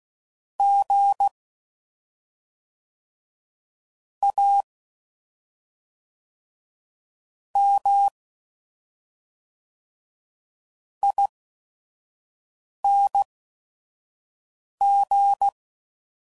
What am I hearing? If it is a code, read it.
Morse code, heard as a keyed tone, about 16 wpm, spaced as for 5 wpm, 800 Hz. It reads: G A M ING
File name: Suffixes / Special ING's / SlowToFaster ING's